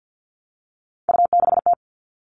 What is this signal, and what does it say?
Morse code, heard as a keyed tone, about 50 wpm, 700 Hz. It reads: U6T
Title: Three Challenging Letter-Number Combinations 50wpm